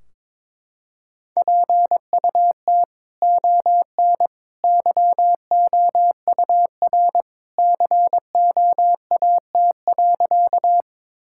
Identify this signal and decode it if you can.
Morse code, heard as a keyed tone, about 22 wpm, 700 Hz. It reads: PUT ON YOUR COAT.
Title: Sentences from Top 1600 Words 22wpm